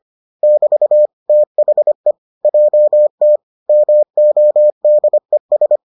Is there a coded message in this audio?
Yes: Morse code from 0.4 to 5.8 s, heard as a keyed tone, about 25 wpm, 600 Hz. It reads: <BT> THE JT MODES